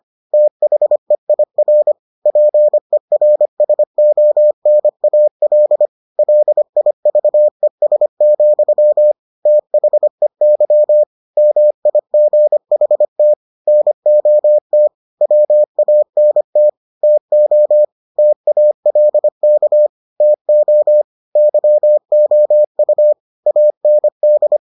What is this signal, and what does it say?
Morse code, heard as a keyed tone, about 25 wpm, 600 Hz. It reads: THEIR PERSONAL LIVES, THEY MIGHT NOT WANT TO TALK TO YOU AND